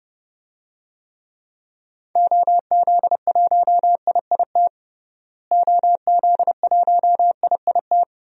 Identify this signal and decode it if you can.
Morse code, heard as a keyed tone, about 30 wpm, 700 Hz. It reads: OZ1IIT OZ1IIT
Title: DXCC-100 Call Signs - 6 Characters - Repeated 3x - 30wpm